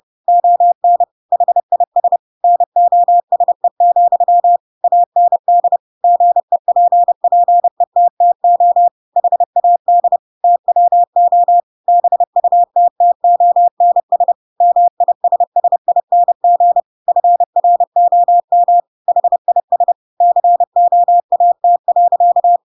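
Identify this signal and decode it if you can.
Morse code, heard as a keyed tone, about 30 wpm, 700 Hz. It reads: ON HIS NOSE, AND GEPPETTO HAD TWO BUTTONS MISSING FROM HIS COAT.
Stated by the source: Chapter 2 of The Adventures of Pinocchio read as a book in Morse code (CW) at 30 wpm, by ik1hge.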